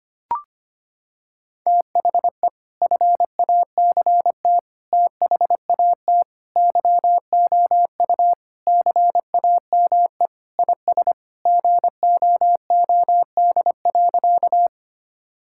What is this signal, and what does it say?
Morse code, heard as a keyed tone, about 25 wpm, 700 Hz. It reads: THE FACT THAT YOU CAME IS GOOD.
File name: practice / 25wpm / Sentences with 3-Word Phrases - 25wpm